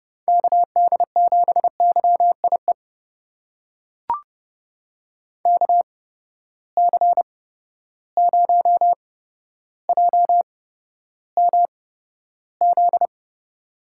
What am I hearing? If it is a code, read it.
Morse code, heard as a keyed tone, about 30 wpm, spaced as for 13 wpm, 700 Hz. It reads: KD7YIE KC0JMZ
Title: USA Call Signs - 6 Characters - 8x Character Spacing - 30wpm